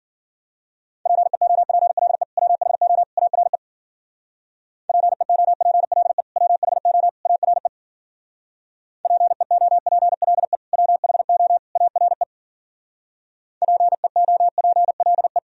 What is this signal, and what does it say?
Morse code, heard as a keyed tone, about 49 wpm, 700 Hz. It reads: PEOPLE WHO ARE PEOPLE WHO ARE PEOPLE WHO ARE PEOPLE
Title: Speed-Racing - Sets of 3 Words 40wpm